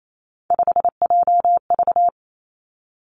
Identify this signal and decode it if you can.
Morse code, heard as a keyed tone, about 28 wpm, 700 Hz. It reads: 5JV
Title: Three Challenging Letter-Number Combinations 28wpm